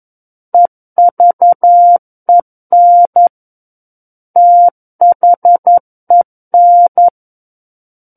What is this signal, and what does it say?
Morse code, heard as a keyed tone, about 11 wpm, 700 Hz. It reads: EVEN THEN
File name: 02-Chapter 2 - 11 WPM